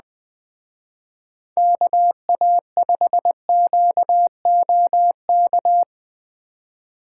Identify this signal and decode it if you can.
Morse code, heard as a keyed tone, about 23 wpm, 700 Hz. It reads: KA5QOK
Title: Speed Racing - USA 2x3 Call Signs 20wpm